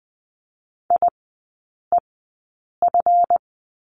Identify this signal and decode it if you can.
Morse code, heard as a keyed tone, about 20 wpm, spaced as for 8 wpm, 700 Hz. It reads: IEF
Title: 145 Lesson 24e - Letter y - Three Characters 20wpm